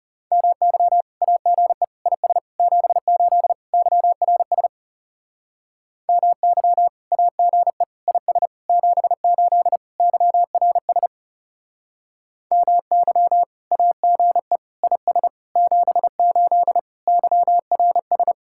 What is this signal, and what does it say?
Morse code, heard as a keyed tone, about 38 wpm, 700 Hz. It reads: MY AGE IS 78 YRS MY AGE IS 78 YRS MY AGE IS 78 YRS